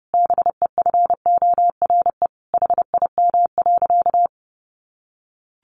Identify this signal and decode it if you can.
Morse code, heard as a keyed tone, about 30 wpm, 700 Hz. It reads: BEFORE HIM.
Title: Sentences from Top 1400 Words 30wpm